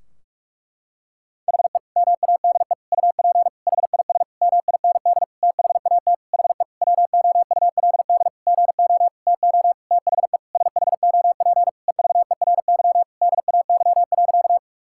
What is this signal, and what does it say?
Morse code, heard as a keyed tone, about 47 wpm, 700 Hz. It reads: HE MADE UP HIS MIND THAT HE WOULD GO TO THE SHOP EVERY DAY.
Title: Sentences from Top 1600 Words 45wpm